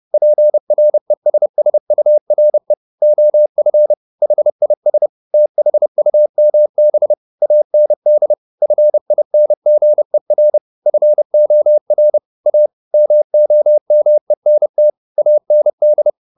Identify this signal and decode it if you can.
Morse code, heard as a keyed tone, about 30 wpm, 600 Hz. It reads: PRESSURE OF HIS THUMB AND FINGER FOR A MOMENT AND
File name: the old man and the sea / the old man and the sea 1 30wpm